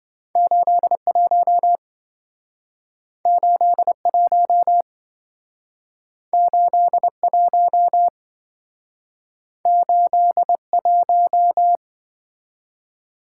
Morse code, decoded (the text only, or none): 81 81 81 81